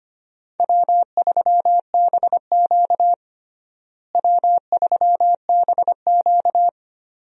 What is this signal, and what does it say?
Morse code, heard as a keyed tone, about 25 wpm, 700 Hz. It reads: W3BQ W3BQ
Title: USA Call Signs - 4 Characters - Repeated 2x - 25wpm